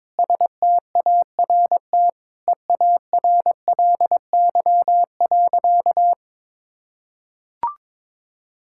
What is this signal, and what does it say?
Morse code, heard as a keyed tone, about 22 wpm, 700 Hz. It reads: START EARLY.
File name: practice / 22wpm / Sentences with 8-Word Phrases - 22wpm